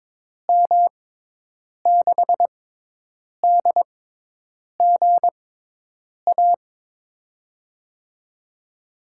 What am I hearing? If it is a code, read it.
Morse code, heard as a keyed tone, about 22 wpm, spaced as for 8 wpm, 700 Hz. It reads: M6DGA
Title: USA Call Signs - 6 Characters - 6x Character Spacing - 22wpm